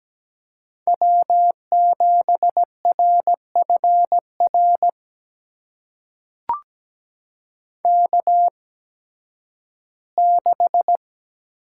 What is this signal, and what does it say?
Morse code, heard as a keyed tone, about 17 wpm, spaced as for 11 wpm, 700 Hz. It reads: W7RFR K6
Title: USA Call Signs - 5 Characters - 8x Character Spacing - 17wpm